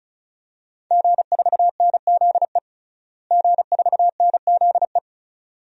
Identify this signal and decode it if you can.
Morse code, heard as a keyed tone, about 36 wpm, 700 Hz. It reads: G4NZE G4NZE
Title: DXCC-100 Call Signs - 5 Characters - Repeated 2x - 35wpm